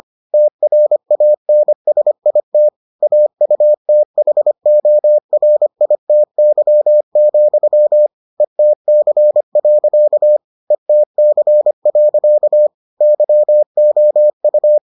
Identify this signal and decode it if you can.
Morse code, heard as a keyed tone, about 25 wpm, 600 Hz. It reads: TRANSIT AUTHORITY, ETC. ETC. YOU